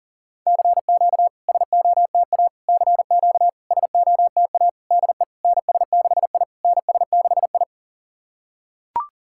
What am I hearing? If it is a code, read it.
Morse code, heard as a keyed tone, about 40 wpm, 690 Hz. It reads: CQ SOTA CQ SOTA DE NS6I NS6I
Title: SOTA QSO Practice 40wpm